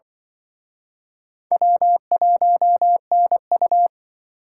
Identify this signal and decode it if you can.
Morse code, heard as a keyed tone, about 25 wpm, 700 Hz. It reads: W1NU